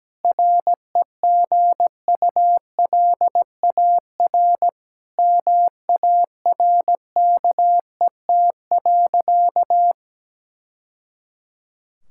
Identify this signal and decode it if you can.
Morse code, heard as a keyed tone, about 17 wpm, 700 Hz. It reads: REGULAR MARKET.
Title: Sentences from Top 1000 Words - Review Part 6 - 17wpm